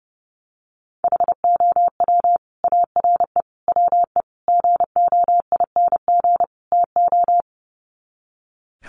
Morse code, heard as a keyed tone, about 30 wpm, 700 Hz. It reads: HOW ARE WE GOING TO
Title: Sets of 5 Words Encore 30wpm